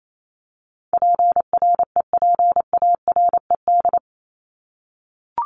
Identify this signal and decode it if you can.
Morse code, heard as a keyed tone, about 28 wpm, 700 Hz. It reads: PREPARED
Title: ED Suffix - Top 100 Words with Base Words 28wpm